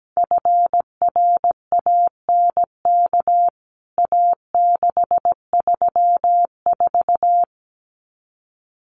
Morse code, heard as a keyed tone, about 17 wpm, 700 Hz. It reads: FRANK A634
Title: CWT Exchange 17wpm